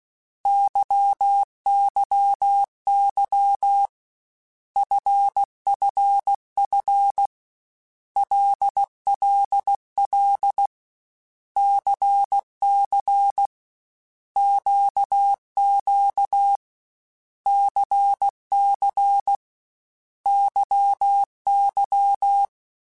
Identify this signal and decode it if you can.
Morse code, heard as a keyed tone, about 16 wpm, 800 Hz. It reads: YYY FFF LLL CC QQ CC YY
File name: Basic Letter Groups By7Groups / LFCYQ / LFCYQ@13